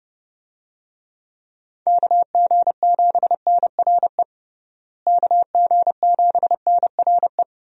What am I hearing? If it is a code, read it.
Morse code, heard as a keyed tone, about 32 wpm, 700 Hz. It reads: KG7NRE KG7NRE
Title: USA Call Signs - 6 Characters - Repeated 3x - 30wpm